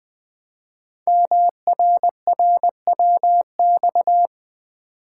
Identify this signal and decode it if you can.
Morse code, heard as a keyed tone, about 20 wpm, 700 Hz. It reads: MRRWX
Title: Code Groups - 5 Letters - 20wpm